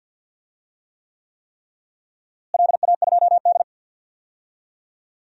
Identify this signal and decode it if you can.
Morse code, heard as a keyed tone, about 50 wpm, 700 Hz. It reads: LA2D
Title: DXCC-100 Call Signs - 4 Characters - Repeated 3x - 50wpm